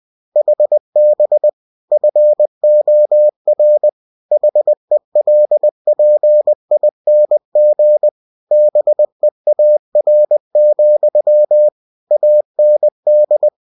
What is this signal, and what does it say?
Morse code, heard as a keyed tone, about 20 wpm, 600 Hz. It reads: HB FOR HELPING BEAR, AND
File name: cw-books / winnie-the-pooh / winnie-20wpm-page-0109